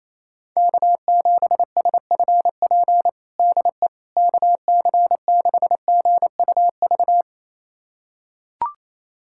Morse code, heard as a keyed tone, about 28 wpm, 700 Hz. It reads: K7SFP DE KC6GUV
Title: USA Callsign DE Callsign 28wpm